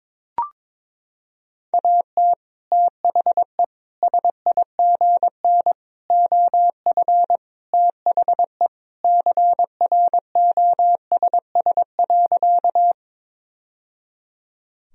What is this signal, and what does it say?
Morse code, heard as a keyed tone, about 22 wpm, 700 Hz. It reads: AT THE SIGN OF THE CROSS.
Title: Sentences from Top 1000 Words - Review Part 1 - 22wpm